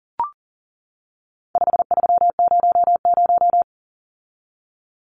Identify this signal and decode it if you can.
Morse code, heard as a keyed tone, about 40 wpm, 700 Hz. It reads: 5300
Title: Numbers - 4-Digit 40wpm